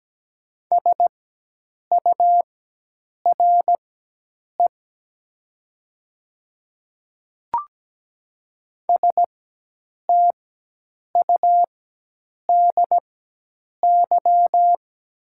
Morse code, heard as a keyed tone, about 17 wpm, spaced as for 8 wpm, 700 Hz. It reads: SURE STUDY
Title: 4x Character Spacing - Top 300 Words - 17wpm